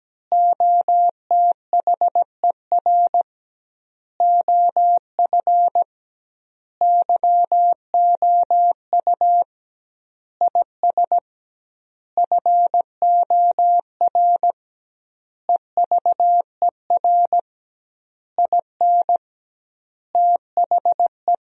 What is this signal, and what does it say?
Morse code, heard as a keyed tone, about 17 wpm, 700 Hz. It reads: OTHER OF YOU IS FOR EVER IN THE
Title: 2x Word-Spacing - Sentences from Top 400 Words - 17wpm